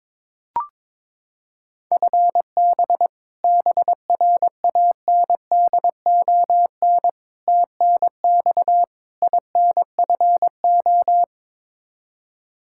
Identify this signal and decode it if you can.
Morse code, heard as a keyed tone, about 22 wpm, 700 Hz. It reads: FB BRANDON TNX INFO
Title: Practice QSO - Exchange 3 - More Info - 22wpm